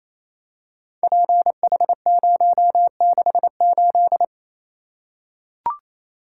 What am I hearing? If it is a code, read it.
Morse code, heard as a keyed tone, about 28 wpm, 700 Hz. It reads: PH068
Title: Code Groups - 5 Letter:Numbers - 28wpm